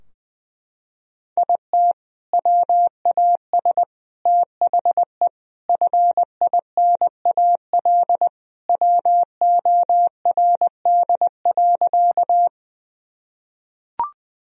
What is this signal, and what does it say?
Morse code, heard as a keyed tone, about 20 wpm, 700 Hz. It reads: IT WAS THE FINAL WORD.
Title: Sentences from Top 1000 Words - Review Part 2 - 20wpm